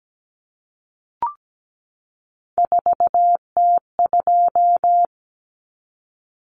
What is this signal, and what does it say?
Morse code, heard as a keyed tone, about 17 wpm, 700 Hz. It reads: 4T2
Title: Common Cut Numbers - 3-Digit 17wpm